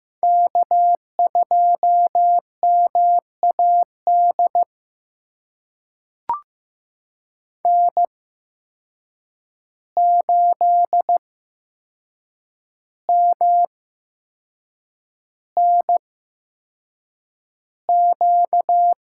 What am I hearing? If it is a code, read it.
Morse code, heard as a keyed tone, about 15 wpm, spaced as for 6 wpm, 700 Hz. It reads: K2MAD N8MNQ
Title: USA Call Signs - 5 Characters - 8x Character Spacing - 15wpm